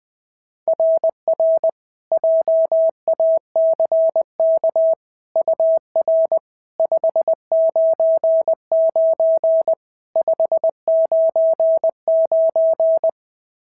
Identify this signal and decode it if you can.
Morse code, heard as a keyed tone, about 18 wpm, 650 Hz. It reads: RR JACK UR 599 599
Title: SOTA QSO Practice 20wpm